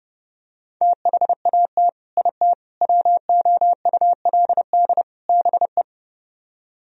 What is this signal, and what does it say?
Morse code, heard as a keyed tone, about 32 wpm, 700 Hz. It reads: THAT IT WOULD BE